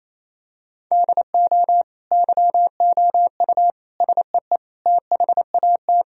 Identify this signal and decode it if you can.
Morse code, heard as a keyed tone, about 28 wpm, 700 Hz. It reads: DO YOU SEE THAT
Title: Sentences from Top 1000 Words - Review Part 1 - 28wpm